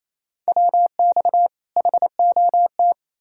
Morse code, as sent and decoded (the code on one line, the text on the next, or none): .-- -..- / .... --- -
WX HOT